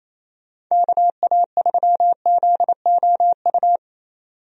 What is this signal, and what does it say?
Morse code, heard as a keyed tone, about 28 wpm, 700 Hz. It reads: KA3ZOU
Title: USA Call Signs - 6 Characters - 28wpm